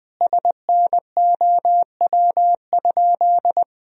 Morse code, heard as a keyed tone, about 21 wpm, 700 Hz. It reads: SNOW?
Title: Sentences from Top 1000 Words - Review Part 3 - 20wpm